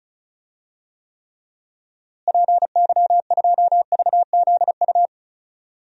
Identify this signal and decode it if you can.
Morse code, heard as a keyed tone, about 34 wpm, 700 Hz. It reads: PY2VZU
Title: DXCC-100 Call Signs - 6 Characters - Repeated 3x - 35wpm